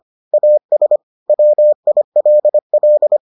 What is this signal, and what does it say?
Morse code, heard as a keyed tone, about 25 wpm, 600 Hz. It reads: AS WILL